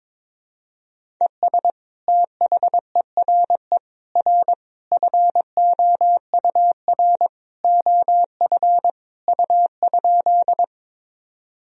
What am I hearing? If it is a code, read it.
Morse code, heard as a keyed tone, about 22 wpm, 700 Hz. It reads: ES THERE R FOUR OF U?
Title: Sentences from Top 300 Words and Abbreviations 22wpm